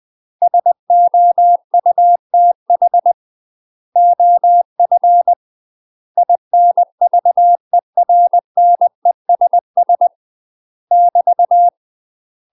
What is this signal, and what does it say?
Morse code, heard as a keyed tone, about 20 wpm, 700 Hz. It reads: SOUTH OF INVERNESS <BT>
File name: UK MASTER QSOs TRAINING / 20WPM / UK MASTER QSOs 0072